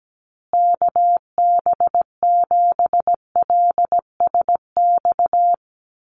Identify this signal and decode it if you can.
Morse code, heard as a keyed tone, about 17 wpm, 700 Hz. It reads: KB7LSX
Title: USA Call Signs Repeated 2x 17wpm